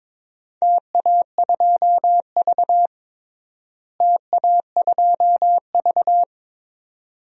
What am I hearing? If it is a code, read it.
Morse code, heard as a keyed tone, about 22 wpm, 700 Hz. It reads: TA2V TA2V
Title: DXCC-100 Call Signs - 4 Characters - Repeated 2x - 22wpm